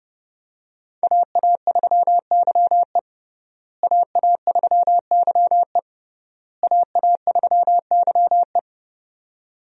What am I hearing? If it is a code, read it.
Morse code, heard as a keyed tone, about 30 wpm, 700 Hz. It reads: AA3YE AA3YE AA3YE
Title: USA Call Signs - 5 Characters - Repeated 3x - 30wpm